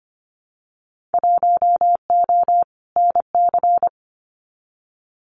Morse code, heard as a keyed tone, about 25 wpm, 700 Hz. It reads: 1O NC
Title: Winter Field Day Exchanges Encore 25wpm